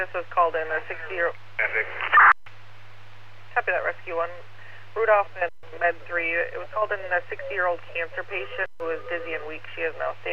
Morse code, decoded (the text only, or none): none